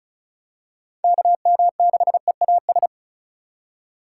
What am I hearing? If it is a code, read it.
Morse code, heard as a keyed tone, about 34 wpm, 700 Hz. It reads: KM6EAS
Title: USA Call Signs - 6 Characters - 4x Character Spacing - 35wpm